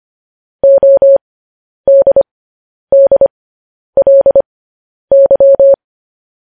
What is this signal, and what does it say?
Morse code, heard as a keyed tone, about 25 wpm, spaced as for 10 wpm, 550 Hz. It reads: ODDLY
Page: news headlines in morse code at 10 wpm